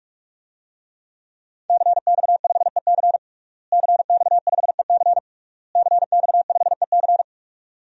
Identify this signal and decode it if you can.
Morse code, heard as a keyed tone, about 41 wpm, 700 Hz. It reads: CX5EC CX5EC CX5EC
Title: DXCC-100 Call Signs - 5 Characters - Repeated 3x - 45wpm